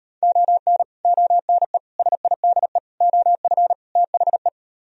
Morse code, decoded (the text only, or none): ON ONE SIDE OF THE